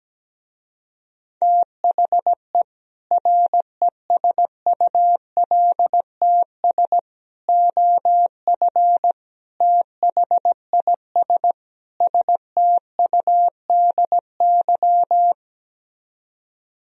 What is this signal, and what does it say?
Morse code, heard as a keyed tone, about 17 wpm, 700 Hz. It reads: THE RESULTS OF THIS STUDY
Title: Speed-Racing - Sets of 5 Words 17wpm